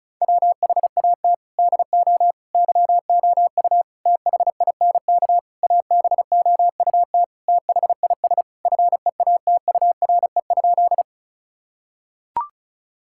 Morse code, heard as a keyed tone, about 36 wpm, 700 Hz. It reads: WHAT DO YOU THINK ABOUT THIS FEATURE?
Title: Sentences from Top 1000 Words - Review Part 5 - 35wpm